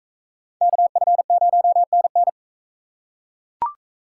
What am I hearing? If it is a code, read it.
Morse code, heard as a keyed tone, about 41 wpm, 700 Hz. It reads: KF0NN